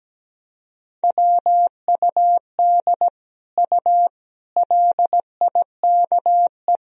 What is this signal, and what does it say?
Morse code, heard as a keyed tone, about 17 wpm, 700 Hz. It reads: WUD U LIKE
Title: Sentences from Top 100 Words and Abbreviations 17wpm